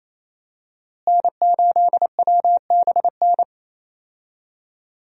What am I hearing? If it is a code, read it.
Morse code, heard as a keyed tone, about 28 wpm, 700 Hz. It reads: N8WBN
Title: USA Call Signs - 5 Characters - Repeated 2x - 28wpm